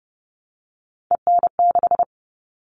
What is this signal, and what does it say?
Morse code, heard as a keyed tone, about 30 wpm, 700 Hz. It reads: EN6
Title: All Cut Numbers - 3-Digit 30wpm